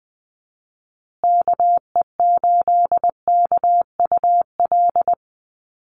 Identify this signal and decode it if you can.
Morse code, heard as a keyed tone, about 20 wpm, 700 Hz. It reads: KE8KUL
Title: USA Call Signs Repeated 3x 20wpm